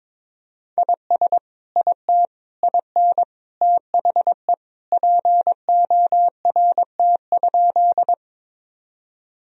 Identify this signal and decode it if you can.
Morse code, heard as a keyed tone, about 22 wpm, 700 Hz. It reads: IS IT IN THE PORT?
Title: Sentences from Top 1000 Words - Review Part 1 - 22wpm